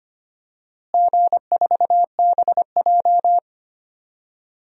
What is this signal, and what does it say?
Morse code, heard as a keyed tone, about 25 wpm, 700 Hz. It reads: G4BJ
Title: DXCC-100 Call Signs - 4 Characters - Repeated 2x - 25wpm